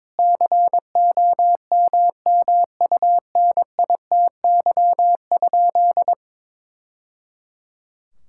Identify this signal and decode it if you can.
Morse code, heard as a keyed tone, about 22 wpm, 700 Hz. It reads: COMMUNITY?
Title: Sentences from Top 1000 Words - Review Part 5 - 22wpm